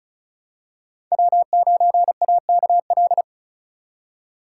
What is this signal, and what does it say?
Morse code, heard as a keyed tone, about 36 wpm, 700 Hz. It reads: W9AKL
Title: USA Call Signs - 5 Characters - 2x Character Spacing - 35wpm